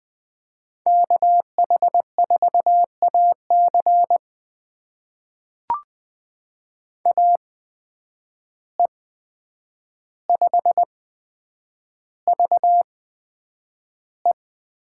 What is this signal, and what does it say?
Morse code, heard as a keyed tone, about 20 wpm, spaced as for 9 wpm, 700 Hz. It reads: KH4AC AE5VE
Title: USA Call Signs - 5 Characters - 8x Character Spacing - 20wpm